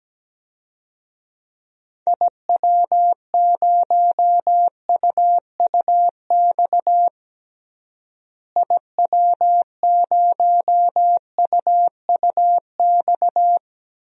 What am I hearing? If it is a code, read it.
Morse code, heard as a keyed tone, about 17 wpm, 700 Hz. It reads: IW0UUX IW0UUX
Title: DXCC-100 Call Signs - 6 Characters - Repeated 2x - 17wpm